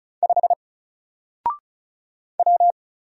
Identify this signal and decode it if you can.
Morse code, heard as a keyed tone, about 36 wpm, 700 Hz. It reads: 5 W
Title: Single Character (Letter:Number:Punctuation:Prosign) Alt 35wpm